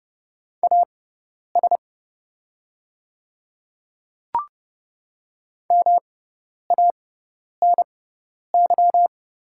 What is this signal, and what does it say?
Morse code, heard as a keyed tone, about 30 wpm, spaced as for 10 wpm, 700 Hz. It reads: AS MANY
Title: 6x Character Spacing - Top 100 Words - 30wpm